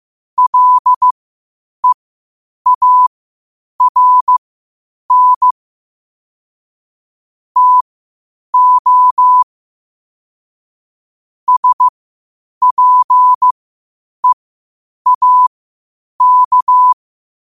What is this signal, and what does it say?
Morse code, heard as a keyed tone, about 15 wpm, spaced as for 8 wpm, 1000 Hz. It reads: LEARN TO SPEAK
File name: qotdfiles / 8wpm